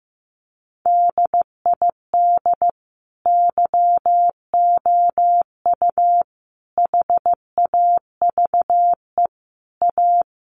Morse code, heard as a keyed tone, about 14 wpm, 700 Hz. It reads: DID YOU HAVE A
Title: Sentences from Top 1100 Words 15wpm